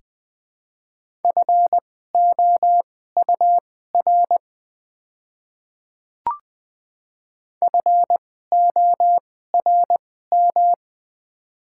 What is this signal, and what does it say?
Morse code, heard as a keyed tone, about 20 wpm, spaced as for 14 wpm, 700 Hz. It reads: FOUR FORM
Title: 2x Character Spacing - Top 300 Words - 20wpm